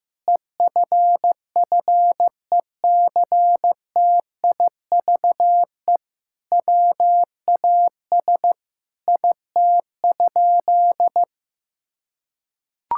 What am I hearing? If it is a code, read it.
Morse code, heard as a keyed tone, about 15 wpm, 700 Hz. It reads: EFFECTIVE WAS IT?
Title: Sentences from Top 1000 Words - Review Part 6 - 15wpm